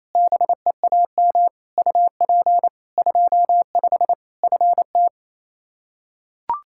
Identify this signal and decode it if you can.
Morse code, heard as a keyed tone, about 28 wpm, 700 Hz. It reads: BEAM UP 25 FT
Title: Practice QSO - Exchange 2 - Info - 28wpm